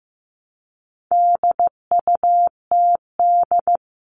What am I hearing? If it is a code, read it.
Morse code, heard as a keyed tone, about 15 wpm, 700 Hz. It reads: DUTD
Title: All Cut Numbers - 4-Digit 15wpm